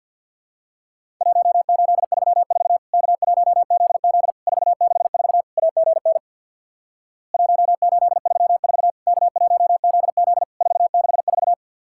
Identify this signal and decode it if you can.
Morse code, heard as a keyed tone, about 50 wpm, 700 Hz. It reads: 1834 K1ZZ 464 AGN 1834 K1ZZ 464
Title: Worked All Europe DX Contest - QTC Exchange with Repeats - 50wpm